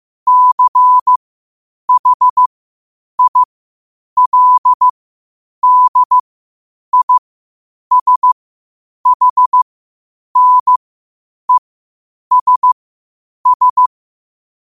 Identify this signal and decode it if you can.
Morse code, heard as a keyed tone, about 15 wpm, spaced as for 8 wpm, 1000 Hz. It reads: CHILDISHNESS
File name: qotdfiles / 8wpm